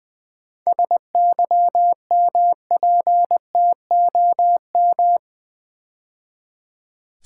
Morse code, heard as a keyed tone, about 20 wpm, 700 Hz. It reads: SYMPTOM
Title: 7 Letter Words - First 3 Letters Given - 20wpm